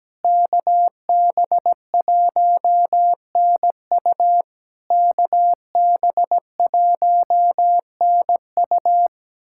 Morse code, heard as a keyed tone, about 17 wpm, 700 Hz. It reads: KB1NU KB1NU
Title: SOTA QSO Practice 17wpm